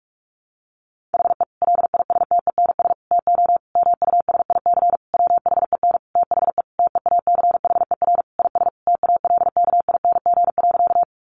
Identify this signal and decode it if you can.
Morse code, heard as a keyed tone, about 44 wpm, 700 Hz. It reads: HE LISTENS TO MUSIC WHEN THE TEACHER IS TALKING.
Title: Additional Easy Phrases 45wpm